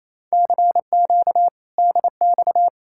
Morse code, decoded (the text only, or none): CQ DX